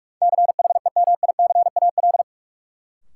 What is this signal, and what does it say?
Morse code, heard as a keyed tone, about 46 wpm, 700 Hz. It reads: CHEMICAL
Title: Sentences from Top 1200 Words - New Word Drill - 45wpm